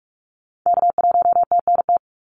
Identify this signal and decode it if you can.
Morse code, heard as a keyed tone, about 44 wpm, 700 Hz. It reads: K1TNT
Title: NA SOTA Activator Callsigns 45wpm